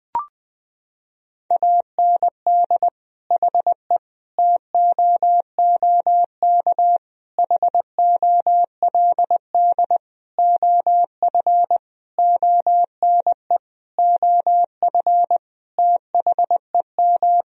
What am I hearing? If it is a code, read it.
Morse code, heard as a keyed tone, about 21 wpm, 700 Hz. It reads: AND HE TOOK HOLD OF ONE OF THEM
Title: Sentences from Top 1000 Words - Review Part 4 - 20wpm